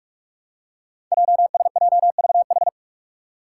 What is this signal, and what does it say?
Morse code, heard as a keyed tone, about 44 wpm, 700 Hz. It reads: JSJVH
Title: Five Challenging Letter-Number Combinations 45wpm